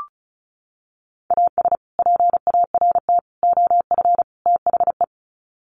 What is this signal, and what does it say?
Morse code, heard as a keyed tone, about 34 wpm, 700 Hz. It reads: AS PART OF THE
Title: Sets of 4 Words 35wpm